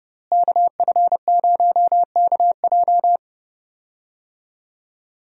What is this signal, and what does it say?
Morse code, heard as a keyed tone, about 30 wpm, 700 Hz. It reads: KF0KJ
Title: USA Call Signs - 5 Characters - Repeated 2x - 30wpm